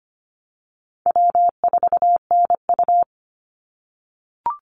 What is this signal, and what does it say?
Morse code, heard as a keyed tone, about 25 wpm, 700 Hz. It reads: W4NU
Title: Easy Call Signs - 25wpm